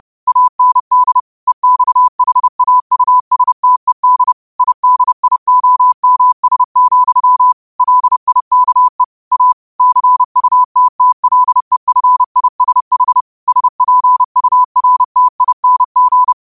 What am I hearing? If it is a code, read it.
Morse code, heard as a keyed tone, about 30 wpm, 1000 Hz. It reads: AND EXHAUSTED IDIOMS, LIKE A CUTTLEFISH SPURTING
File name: qotdfiles / 30wpm